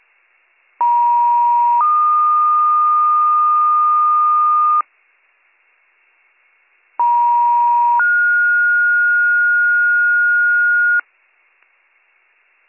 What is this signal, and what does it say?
Two-tone page, heard as a one-second tone, then a three-second tone, about 950 then 1220 Hz, about 950 then 1430 Hz. It is heard from 0.8 to 11.0 s.